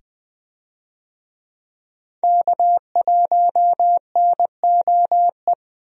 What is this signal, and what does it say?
Morse code, heard as a keyed tone, about 21 wpm, 700 Hz. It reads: K1NOE
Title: USA Call Signs - 5 Characters - 20wpm